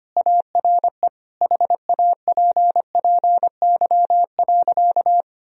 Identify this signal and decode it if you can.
Morse code, heard as a keyed tone, about 25 wpm, 700 Hz. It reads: ARE HAPPY.
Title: Sentences from Top 1000 Words - Review Part 3 - 25wpm